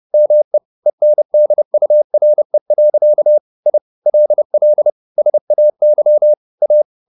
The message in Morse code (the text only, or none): ME ENDURE. I LL SAY A